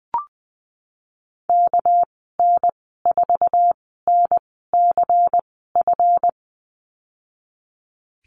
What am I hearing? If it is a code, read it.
Morse code, heard as a keyed tone, about 20 wpm, spaced as for 15 wpm, 700 Hz. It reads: KN4NCF
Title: 213 Lesson 34g - Number 8 - Call Signs 20wpm